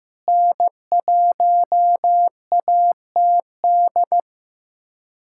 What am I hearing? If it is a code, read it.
Morse code, heard as a keyed tone, about 15 wpm, 700 Hz. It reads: N1ATD
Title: USA Callsign DE Callsign 15wpm